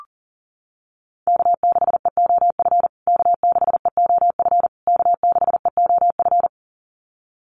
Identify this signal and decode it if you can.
Morse code, heard as a keyed tone, about 40 wpm, 700 Hz. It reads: K6EOF K6EOF K6EOF
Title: USA Call Signs Repeated 3x 40wpm